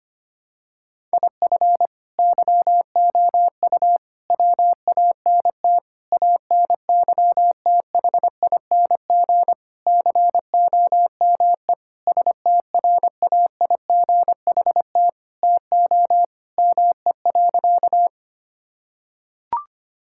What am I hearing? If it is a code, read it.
Morse code, heard as a keyed tone, about 25 wpm, 700 Hz. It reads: IF YOU WANT ANYTHING COME STRAIGHT TO ME.
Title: Sentences from Top 1000 Words - Review Part 6 - 25wpm